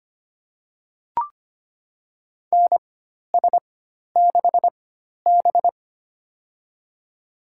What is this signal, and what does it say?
Morse code, heard as a keyed tone, about 25 wpm, spaced as for 12 wpm, 700 Hz. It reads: NS6B
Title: USA Call Signs - 4 Characters - 4x Character Spacing - 25wpm